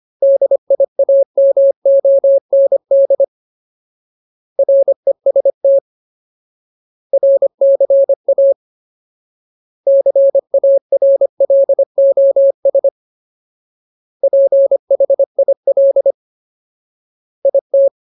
Morse code, heard as a keyed tone, about 25 wpm, 550 Hz. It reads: DIAMOND REST RCA CARLOS PHIL IT
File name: Ultimate Word List / 25WPM Ultimate